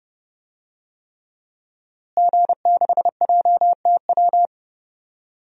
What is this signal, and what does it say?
Morse code, heard as a keyed tone, about 30 wpm, 700 Hz. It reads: G6JTW
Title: DXCC-100 Call Signs - 5 Characters - Repeated 3x - 30wpm